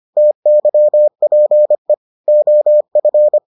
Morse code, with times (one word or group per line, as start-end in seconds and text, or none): 0.2-1.9 TYPE
2.3-3.4 OF